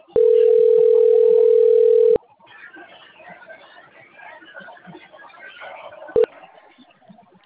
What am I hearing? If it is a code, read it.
Ringback tone (North America), 440 + 480 Hz.
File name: rg-1600-unknown- - - .22691